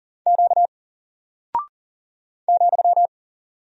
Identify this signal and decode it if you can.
Morse code, heard as a keyed tone, about 40 wpm, 700 Hz. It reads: Q ,
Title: Single Character (Letter:Number:Punctuation:Prosign) Alt 40wpm